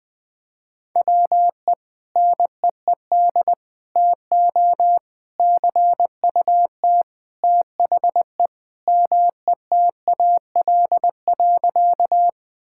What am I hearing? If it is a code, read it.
Morse code, heard as a keyed tone, about 20 wpm, 700 Hz. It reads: WE NEED TO CUT THE METAL.
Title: Sentences from Top 1000 Words - Review Part 6 - 20wpm